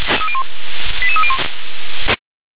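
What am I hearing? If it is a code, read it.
Five-tone selective call (ZVEI), heard as quick runs of five tones, tones of about 70 ms. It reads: 88301
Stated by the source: ZVEI five-tone selective call 88926, received by SDR.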